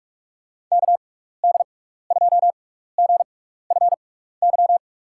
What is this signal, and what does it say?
Morse code, heard as a keyed tone, about 46 wpm, 700 Hz. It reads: K D 2 G F Y
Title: USA Call Signs - 6 Characters - 6x Character Spacing - 45wpm